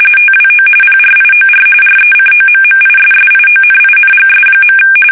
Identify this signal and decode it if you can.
RTTY, about 45.45 baud. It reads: TWO PRIOR RECOMMISSIONINGS.
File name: fsk850 / battleshipnj